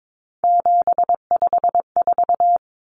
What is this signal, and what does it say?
Morse code, heard as a keyed tone, about 22 wpm, 700 Hz. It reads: 754
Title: Common Cut Numbers - 3-Digit 22wpm